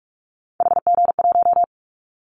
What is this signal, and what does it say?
Morse code, heard as a keyed tone, about 46 wpm, 700 Hz. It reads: HG1
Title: Callsign Prefixes for Top 50 DXCC - 45wpm